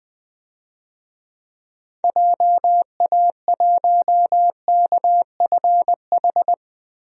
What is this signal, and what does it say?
Morse code, heard as a keyed tone, about 21 wpm, 700 Hz. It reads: JA1KFH
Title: DXCC-100 Call Signs - 6 Characters - Repeated 3x - 20wpm